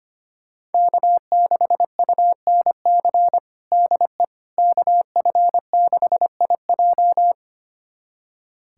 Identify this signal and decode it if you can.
Morse code, heard as a keyed tone, about 25 wpm, 700 Hz. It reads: K6UNC DE KF6IJ
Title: USA Callsign DE Callsign - 5 Characters - 25wpm